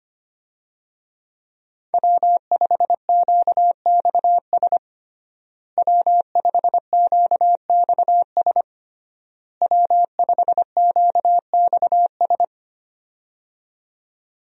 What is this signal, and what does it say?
Morse code, heard as a keyed tone, about 25 wpm, 700 Hz. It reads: W5QXS W5QXS W5QXS
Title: USA Call Signs - 5 Characters - Repeated 3x - 25wpm